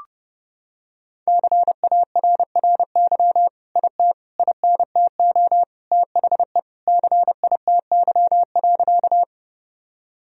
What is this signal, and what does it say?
Morse code, heard as a keyed tone, about 30 wpm, 700 Hz. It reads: CARRY IT INTO THE CITY.
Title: Sentences from Top 1000 Words - Review Part 1 - 30wpm